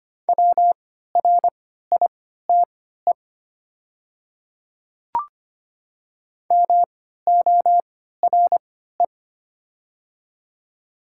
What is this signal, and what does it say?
Morse code, heard as a keyed tone, about 25 wpm, spaced as for 14 wpm, 700 Hz. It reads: WRITE MORE
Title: 3x Character Spacing - Top 100 Words - 25wpm